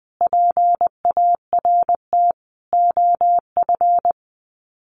Morse code, heard as a keyed tone, about 20 wpm, 700 Hz. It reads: PART OF
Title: Sets of 5 Words Encore2 20wpm